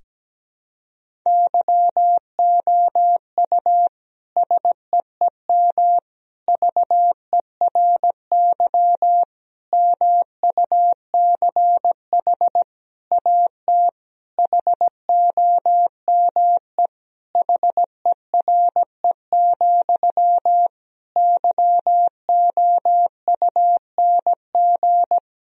Morse code, heard as a keyed tone, about 17 wpm, 700 Hz. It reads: YOU SEEM VERY MUCH AT HOME HERE, YOUNG
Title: Sentences from Top 1000 Words - Review Part 1 - 17wpm